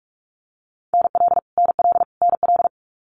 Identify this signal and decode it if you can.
Morse code, heard as a keyed tone, about 44 wpm, 700 Hz. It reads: NL NL NL
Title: Canadian Provinces and Territories - Repeated 3x - 45wpm